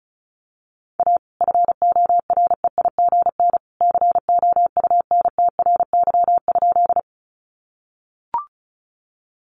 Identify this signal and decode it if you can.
Morse code, heard as a keyed tone, about 34 wpm, 700 Hz. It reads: A FOREIGN COUNTRY?
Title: Sentences from Top 1400 Words 35wpm